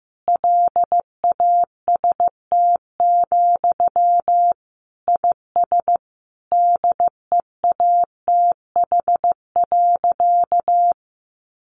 Morse code, heard as a keyed tone, about 15 wpm, 700 Hz. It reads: LAST, IS DEATH.
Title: Sentences from Top 1100 Words 15wpm